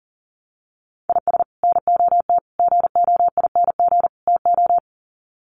Morse code, heard as a keyed tone, about 37 wpm, 700 Hz. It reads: IS NOT GOING TO
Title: Sets of 4 Words 40wpm